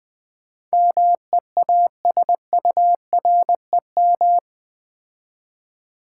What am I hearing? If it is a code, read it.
Morse code, heard as a keyed tone, about 20 wpm, 700 Hz. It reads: MEASUREM
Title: Build A Word - 4 Letters At A Time - 11-Letter Words - 20wpm